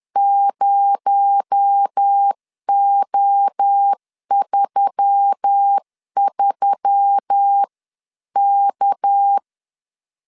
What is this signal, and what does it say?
Morse code, heard as a keyed tone, about 10 wpm, 800 Hz. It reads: 0O33K
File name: CW / DET / 1 6 1500C S C 10wpm 4252